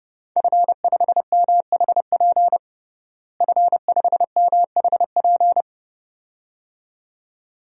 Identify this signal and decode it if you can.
Morse code, heard as a keyed tone, about 30 wpm, 700 Hz. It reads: F5MHP F5MHP
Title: DXCC-100 Call Signs - 5 Characters - Repeated 2x - 30wpm